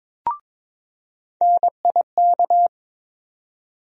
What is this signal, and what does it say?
Morse code, heard as a keyed tone, about 22 wpm, 700 Hz. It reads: NIK
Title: USA Call Sign Postfixes - 3 Letters 22wpm